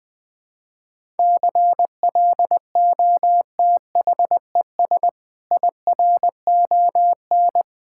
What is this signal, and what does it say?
Morse code, heard as a keyed tone, about 20 wpm, 700 Hz. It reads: CLOTHES IRON
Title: Difficult Words - Speed-Racing with Hints - 20wpm